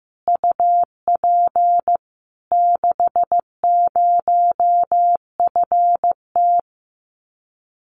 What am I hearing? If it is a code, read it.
Morse code, heard as a keyed tone, about 15 wpm, 700 Hz. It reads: UP 60FT
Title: Common QSO Phrases Encore 15wpm